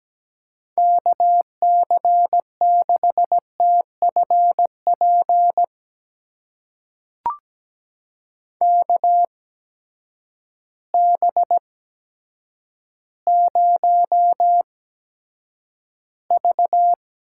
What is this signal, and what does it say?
Morse code, heard as a keyed tone, about 17 wpm, spaced as for 8 wpm, 700 Hz. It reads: KC6TFP KB0V
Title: USA Call Signs - 6 Characters - 8x Character Spacing - 17wpm